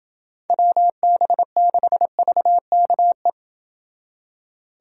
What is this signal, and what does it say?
Morse code, heard as a keyed tone, about 28 wpm, 700 Hz. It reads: WB6VKE